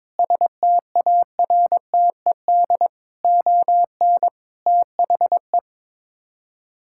Morse code, heard as a keyed tone, about 21 wpm, 700 Hz. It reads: STARTED ON THE
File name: practice / 22wpm / Sets of 8 Words - 22wpm